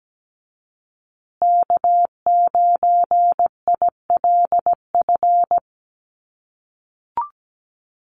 Morse code, heard as a keyed tone, about 17 wpm, 700 Hz. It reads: K9ILF
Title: USA Call Signs Repeated 3x 17wpm